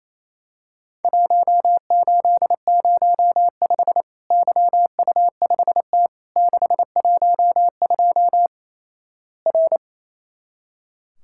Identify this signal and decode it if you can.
Morse code, heard as a keyed tone, about 28 wpm, 700 Hz. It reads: 1805 YU5T 612 R
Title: Worked All Europe DX Contest - QTC Exchange - 28wpm